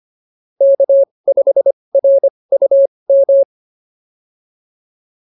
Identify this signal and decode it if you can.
Morse code, heard as a keyed tone, about 25 wpm, spaced as for 20 wpm, 550 Hz. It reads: K5RUM